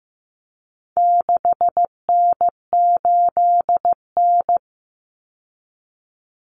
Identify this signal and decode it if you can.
Morse code, heard as a keyed tone, about 15 wpm, 700 Hz. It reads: 6N8N
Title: Common Cut Numbers - 4-Digit 15wpm